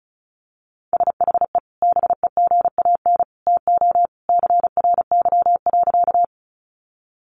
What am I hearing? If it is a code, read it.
Morse code, heard as a keyed tone, about 34 wpm, 700 Hz. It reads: SHE BEGAN TO CRY.
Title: Sentences from Top 500 Words 35wpm